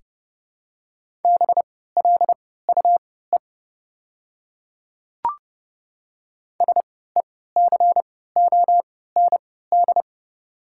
Morse code, heard as a keyed tone, about 32 wpm, spaced as for 17 wpm, 700 Hz. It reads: BLUE SECOND
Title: 3x Character Spacing - Top 500 Words - 30wpm